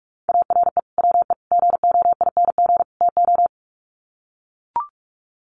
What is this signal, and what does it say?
Morse code, heard as a keyed tone, about 44 wpm, 700 Hz. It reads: ARE WE GOING TO
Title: Sets of 4 Words 45wpm